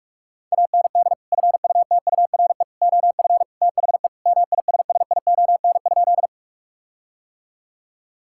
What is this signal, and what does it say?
Morse code, heard as a keyed tone, about 42 wpm, 700 Hz. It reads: AND FUTURE OF THE MISSION?
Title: Sentences from Top 1600 Words 45wpm